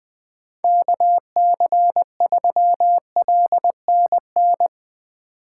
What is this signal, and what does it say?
Morse code, heard as a keyed tone, about 21 wpm, 700 Hz. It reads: KC3LNN
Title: USA Call Signs - 6 Characters - 2x Character Spacing - 20wpm